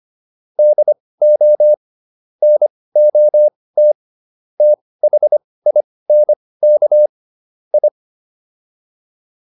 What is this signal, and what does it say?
Morse code, heard as a keyed tone, about 25 wpm, spaced as for 18 wpm, 600 Hz. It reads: DO NOT THINK I
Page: intermediate-practice-files